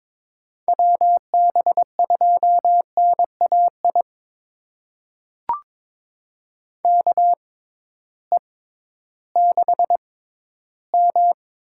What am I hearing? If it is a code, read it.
Morse code, heard as a keyed tone, about 22 wpm, spaced as for 13 wpm, 700 Hz. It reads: WB2NAI KE6M
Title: USA Call Signs - 6 Characters - 6x Character Spacing - 22wpm